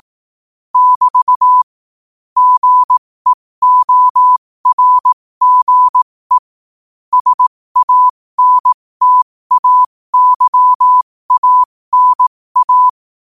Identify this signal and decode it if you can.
Morse code, heard as a keyed tone, about 18 wpm, spaced as for 15 wpm, 1000 Hz. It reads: <BT> GEORGE SANTAYANA
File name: 15wpm